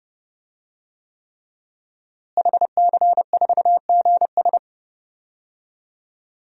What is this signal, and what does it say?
Morse code, heard as a keyed tone, about 30 wpm, 700 Hz. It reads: HC4GS
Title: DXCC-100 Call Signs - 5 Characters - Repeated 3x - 30wpm